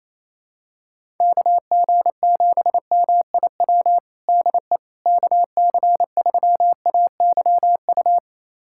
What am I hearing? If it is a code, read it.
Morse code, heard as a keyed tone, about 28 wpm, 700 Hz. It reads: KG7MIW DE KC3AYU
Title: USA Callsign DE Callsign 28wpm